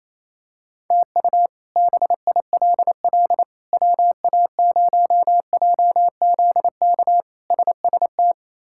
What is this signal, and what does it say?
Morse code, heard as a keyed tone, about 28 wpm, 700 Hz. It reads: TU BILL WA0JZK SST
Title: K1USN SST QSO Practice - 28wpm